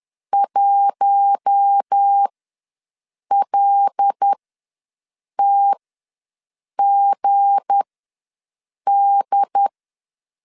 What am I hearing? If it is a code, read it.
Morse code, heard as a keyed tone, about 10 wpm, spaced as for 6 wpm, 800 Hz. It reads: 1LTGD
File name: LEZIONI / 8 YJ45 / 8 YJ4 ws 10wpm 4020